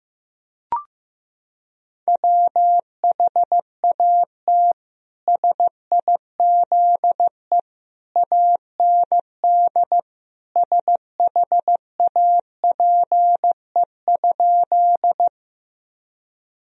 Morse code, heard as a keyed tone, about 15 wpm, 700 Hz. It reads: WHAT SIZE AND SHAPE?
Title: Sentences from Top 1000 Words - Review Part 3 - 15wpm